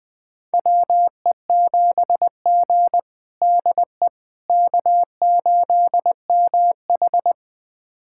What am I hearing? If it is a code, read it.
Morse code, heard as a keyed tone, about 20 wpm, 700 Hz. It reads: WE7G DE K8MH
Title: USA Callsign DE Callsign - 4 Characters - 20wpm